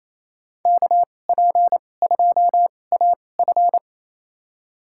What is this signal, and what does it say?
Morse code, heard as a keyed tone, about 28 wpm, spaced as for 20 wpm, 700 Hz. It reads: KP2AF
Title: USA Call Signs - 5 Characters - 2x Character Spacing - 28wpm